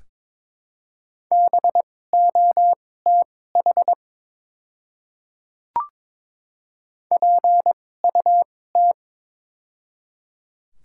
Morse code, heard as a keyed tone, about 22 wpm, spaced as for 16 wpm, 700 Hz. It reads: BOTH PUT
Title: 2x Character Spacing - Top 300 Words - 22wpm